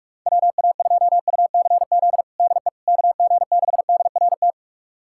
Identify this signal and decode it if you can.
Morse code, heard as a keyed tone, about 46 wpm, 700 Hz. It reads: WA2UCZ DE KG6DRT
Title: USA Callsign DE Callsign - 6 Characters - 45wpm